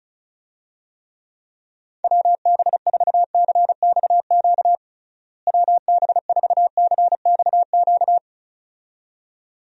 Morse code, heard as a keyed tone, about 36 wpm, 700 Hz. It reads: WB4CXQ WB4CXQ
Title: USA Call Signs - 6 Characters - Repeated 2x - 35wpm